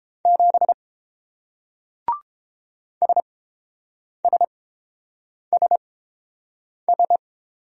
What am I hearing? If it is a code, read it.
Morse code, heard as a keyed tone, about 30 wpm, spaced as for 7 wpm, 690 Hz. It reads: 7 SSSS